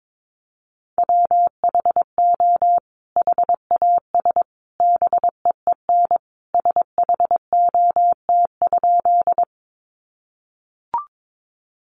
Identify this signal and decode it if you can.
Morse code, heard as a keyed tone, about 22 wpm, 700 Hz. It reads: WHO HAS BEEN SHOT?
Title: Sentences from Top 1100 Words 22wpm